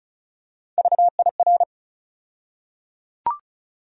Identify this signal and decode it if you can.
Morse code, heard as a keyed tone, about 36 wpm, 700 Hz. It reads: VIR
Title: USA Call Sign Postfixes - 3 Letters 35wpm